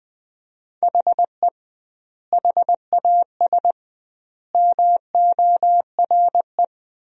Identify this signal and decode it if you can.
Morse code, heard as a keyed tone, about 20 wpm, 700 Hz. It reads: HE HAS MORE
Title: 2x Word-Spacing - Sentences from Top 200 Words - 20wpm